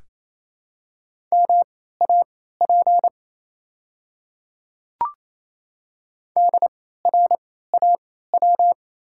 Morse code, heard as a keyed tone, about 28 wpm, spaced as for 16 wpm, 690 Hz. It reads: MAP DRAW
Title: 3x Character Spacing - Top 500 Words - 28wpm